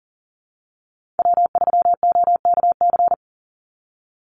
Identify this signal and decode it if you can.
Morse code, heard as a keyed tone, about 40 wpm, 700 Hz. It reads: W3OKC
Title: Call Signs Repeated 2x - 40wpm